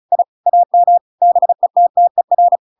Morse code, heard as a keyed tone, about 34 wpm, 700 Hz. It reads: I AM BETTER
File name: Pinocchio / Pinocchio ch8 35 35 ik1hge